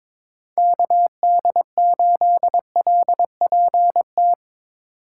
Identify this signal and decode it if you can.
Morse code, heard as a keyed tone, about 22 wpm, 700 Hz. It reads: KD8LPT
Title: USA Call Signs - 6 Characters - 2x Character Spacing - 22wpm